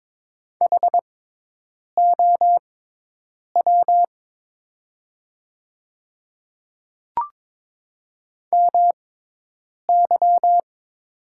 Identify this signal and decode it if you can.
Morse code, heard as a keyed tone, about 22 wpm, spaced as for 8 wpm, 700 Hz. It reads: HOW MY
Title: 6x Character Spacing - Top 100 Words - 22wpm